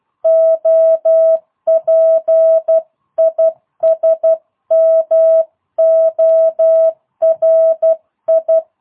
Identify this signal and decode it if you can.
Morse code, heard as a keyed tone, about 12 wpm, 650 Hz. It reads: OP IS MORI
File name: jh8cbh / 2022cw